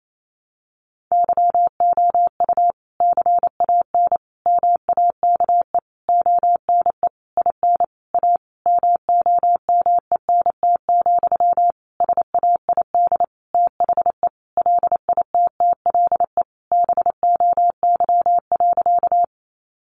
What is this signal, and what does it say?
Morse code, heard as a keyed tone, about 28 wpm, 700 Hz. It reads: YOU CAN MAKE ONE IN A MOMENT, SAID THE LITTLE BOY.
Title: Sentences from Top 1000 Words 28wpm